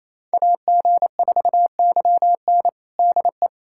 Morse code, heard as a keyed tone, about 27 wpm, 700 Hz. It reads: AG4YN DE
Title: USA Callsign DE Callsign - 5 Characters - 28wpm